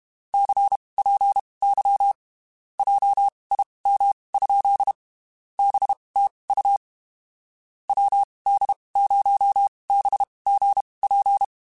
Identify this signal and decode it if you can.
Morse code, heard as a keyed tone, about 32 wpm, spaced as for 23 wpm, 800 Hz. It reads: CPY JIM? BTU WD0BGP
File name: (spaced) / QSOs Spaced-mp3s / WD0BGP 1 @30 sp